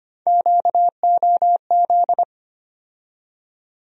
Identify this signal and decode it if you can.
Morse code, heard as a keyed tone, about 25 wpm, 700 Hz. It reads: QOZ
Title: USA Call Sign Postfixes - 3 Letters 25wpm